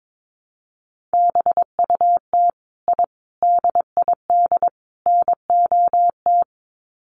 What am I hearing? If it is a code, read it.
Morse code, heard as a keyed tone, about 22 wpm, 700 Hz. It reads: BUT I DID NOT
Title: Sets of 4 Words 22wpm